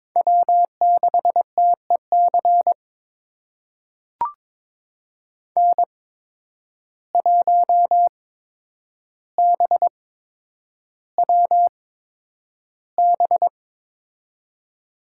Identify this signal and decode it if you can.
Morse code, heard as a keyed tone, about 22 wpm, spaced as for 9 wpm, 700 Hz. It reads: W6TEC N1BWB
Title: USA Call Signs - 5 Characters - 8x Character Spacing - 22wpm